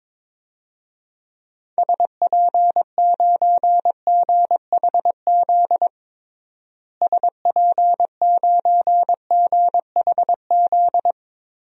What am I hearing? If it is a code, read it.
Morse code, heard as a keyed tone, about 22 wpm, 700 Hz. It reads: SP9GHZ SP9GHZ
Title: DXCC-100 Call Signs - 6 Characters - Repeated 2x - 22wpm